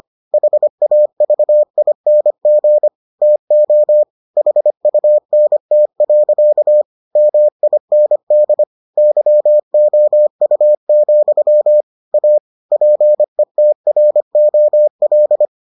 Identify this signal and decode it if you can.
Morse code, heard as a keyed tone, about 25 wpm, 600 Hz. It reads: HAVING TO HUNT. MIND YOU, A PETROL